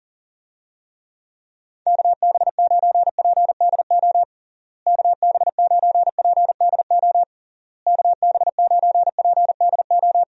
Morse code, heard as a keyed tone, about 40 wpm, 700 Hz. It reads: KB9PDO KB9PDO KB9PDO
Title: USA Call Signs - 6 Characters - Repeated 3x - 40wpm